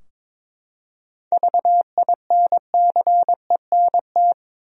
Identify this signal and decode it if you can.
Morse code, heard as a keyed tone, about 22 wpm, 700 Hz. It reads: VINCENT
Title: USA Common Names - 22wpm